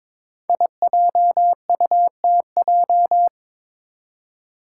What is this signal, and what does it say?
Morse code, heard as a keyed tone, about 23 wpm, 700 Hz. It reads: IJUTJ
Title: Five Challenging Letter-Number Combinations 22wpm